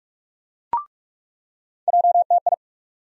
Morse code, heard as a keyed tone, about 43 wpm, 700 Hz. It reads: JTI
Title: USA Call Sign Postfixes - 3 Letters 45wpm